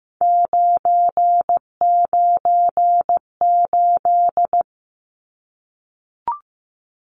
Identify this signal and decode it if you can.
Morse code, heard as a keyed tone, about 15 wpm, 700 Hz. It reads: 998
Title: Numbers - 3-Digit 15wpm